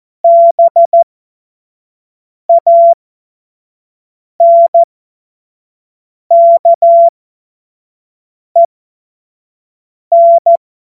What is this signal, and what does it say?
Morse code, heard as a keyed tone, about 14 wpm, spaced as for 5 wpm, 680 Hz. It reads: BANKEN
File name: Morsekurs / Witze / Witz 16. Mai 2023 25 F70